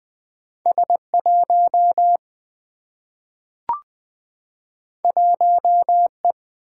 Morse code, heard as a keyed tone, about 20 wpm, 700 Hz. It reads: S1 1E
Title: Single-Letter Single-Number 20wpm